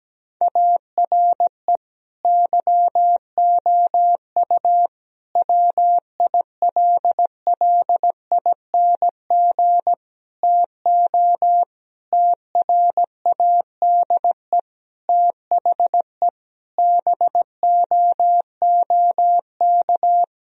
Morse code, heard as a keyed tone, about 17 wpm, 700 Hz. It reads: ARE YOU WILLING TO TRADE THE BOOK
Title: Sentences from Top 1000 Words - Review Part 4 - 17wpm